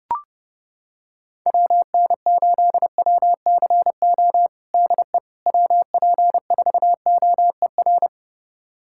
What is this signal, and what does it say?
Morse code, heard as a keyed tone, about 30 wpm, 700 Hz. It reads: WN8WCO DE WP4OER
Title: USA Callsign DE Callsign - 6 Characters - 30wpm